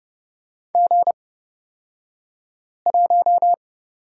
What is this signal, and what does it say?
Morse code, heard as a keyed tone, about 30 wpm, spaced as for 5 wpm, 700 Hz. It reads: G1
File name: Lesson 37d - Letter x - Two Characters 30wpm